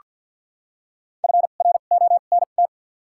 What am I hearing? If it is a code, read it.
Morse code, heard as a keyed tone, about 48 wpm, spaced as for 35 wpm, 700 Hz. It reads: FRONT
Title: 2x Character Spacing - Top 500 Words - 50wpm